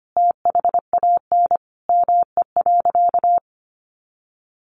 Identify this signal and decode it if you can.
Morse code, heard as a keyed tone, about 25 wpm, 700 Hz. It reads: THAN ME.
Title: Sentences from Top 400 Words 25wpm